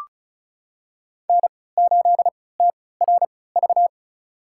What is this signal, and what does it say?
Morse code, heard as a keyed tone, about 34 wpm, spaced as for 20 wpm, 700 Hz. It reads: N8TRV
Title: USA Call Signs - 5 Characters - 3x Character Spacing - 35wpm